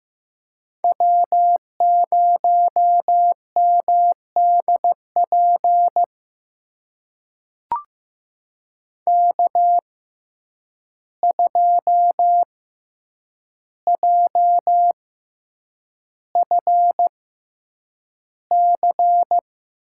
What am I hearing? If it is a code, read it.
Morse code, heard as a keyed tone, about 15 wpm, spaced as for 8 wpm, 700 Hz. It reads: W0MDP K2JFC
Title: USA Call Signs - 5 Characters - 6x Character Spacing - 15wpm